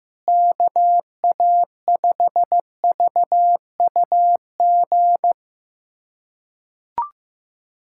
Morse code, heard as a keyed tone, about 15 wpm, 700 Hz. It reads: KA5VUG
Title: USA Callsign DE Callsign - 6 Characters - 15wpm